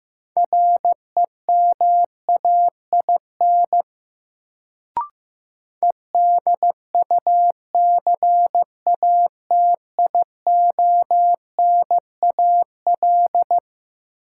Morse code, heard as a keyed tone, about 15 wpm, 700 Hz. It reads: REMAIN EDUCATIONAL
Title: Sentences from Top 1200 Words - New Word Drill - 15wpm